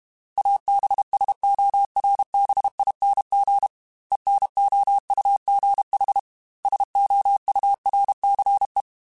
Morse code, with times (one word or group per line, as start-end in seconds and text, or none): 0.4-3.7 ABSORBING
4.1-6.2 ENOUGH
6.7-8.8 SOURCE